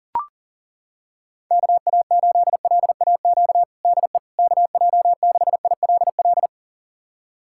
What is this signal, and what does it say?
Morse code, heard as a keyed tone, about 40 wpm, 700 Hz. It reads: KA8LAQ DE KJ6ILL
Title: USA Callsign DE Callsign 40wpm